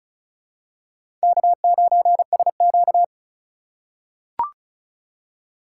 Morse code, heard as a keyed tone, about 36 wpm, 700 Hz. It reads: K9SQ
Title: USA Call Signs - 4 Characters - 3x Character Spacing - 35wpm